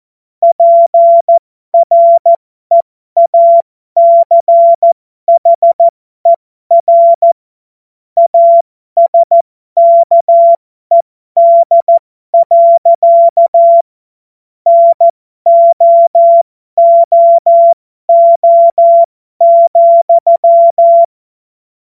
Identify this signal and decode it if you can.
Morse code, heard as a keyed tone, about 14 wpm, spaced as for 12 wpm, 680 Hz. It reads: PREACHER ASKED. NOOO,